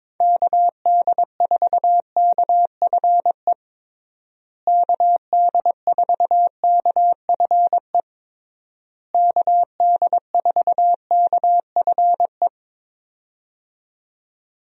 Morse code, heard as a keyed tone, about 22 wpm, 700 Hz. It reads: KD4KFE KD4KFE KD4KFE
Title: USA Call Signs - 6 Characters - Repeated 3x - 22wpm